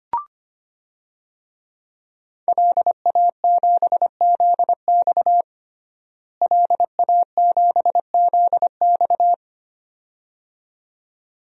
Morse code, heard as a keyed tone, about 25 wpm, 700 Hz. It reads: LA7ZX LA7ZX
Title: DXCC-100 Call Signs - 5 Characters - Repeated 2x - 25wpm